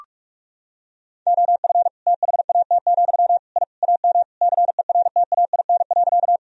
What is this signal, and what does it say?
Morse code, heard as a keyed tone, about 42 wpm, 700 Hz. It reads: OF THAT, I AM CERTAIN.
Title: Sentences from Top 1000 Words - Review Part 2 - 45wpm